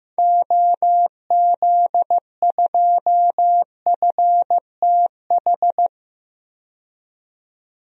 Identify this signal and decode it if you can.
Morse code, heard as a keyed tone, about 15 wpm, 700 Hz. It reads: OZ2FTH
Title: DXCC-100 Call Signs - 6 Characters - 15wpm